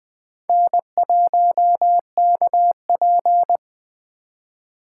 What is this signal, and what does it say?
Morse code, heard as a keyed tone, about 20 wpm, 700 Hz. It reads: N1KP
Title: Speed Racing - USA 1x2 Call Signs 17wpm